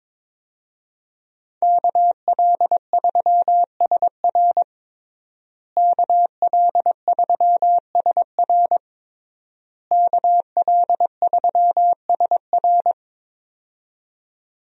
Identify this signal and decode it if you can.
Morse code, heard as a keyed tone, about 22 wpm, 700 Hz. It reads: KL3SR KL3SR KL3SR
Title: USA Call Signs - 5 Characters - Repeated 3x - 22wpm